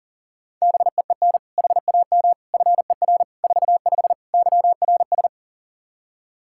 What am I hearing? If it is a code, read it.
Morse code, heard as a keyed tone, about 37 wpm, 700 Hz. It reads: BEEN HAM FER 45 YRS
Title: K1USN SST QSO Practice 40wpm